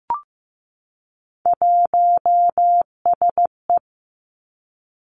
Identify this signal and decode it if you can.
Morse code, heard as a keyed tone, about 15 wpm, 700 Hz. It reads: 1SE
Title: Three Challenging Letter-Number Combinations 15wpm